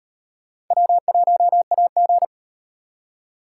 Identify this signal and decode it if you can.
Morse code, heard as a keyed tone, about 38 wpm, 700 Hz. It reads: W1AG